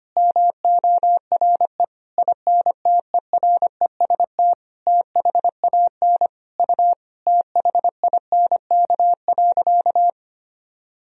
Morse code, heard as a keyed tone, about 25 wpm, 700 Hz. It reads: MORE INTEREST THAN U THINK.
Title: Sentences from Top 400 Words and Abbreviations 25wpm